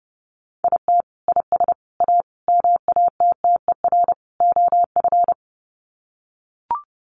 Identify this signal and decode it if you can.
Morse code, heard as a keyed tone, about 30 wpm, 700 Hz. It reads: IT IS A MATTER OF
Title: Sets of 5 Words Encore2 30wpm